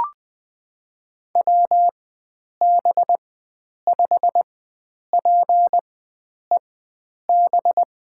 Morse code, heard as a keyed tone, about 20 wpm, spaced as for 9 wpm, 700 Hz. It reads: WB5PEB
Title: USA Call Signs - 6 Characters - 4x Character Spacing - 20wpm